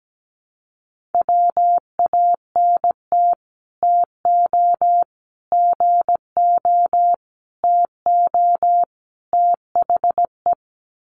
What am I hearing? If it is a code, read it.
Morse code, heard as a keyed tone, about 17 wpm, 700 Hz. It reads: WANT TO GO TO THE
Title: Sets of 5 Words Encore2 17wpm